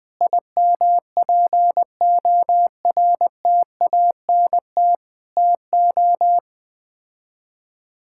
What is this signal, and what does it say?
Morse code, heard as a keyed tone, about 20 wpm, 700 Hz. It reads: IMPORTANT TO
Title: Build A Phrase - 4-Word Phrases - 20wpm